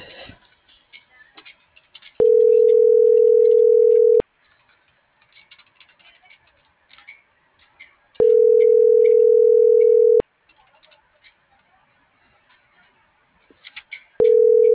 Ringback tone (North America), 440 + 480 Hz.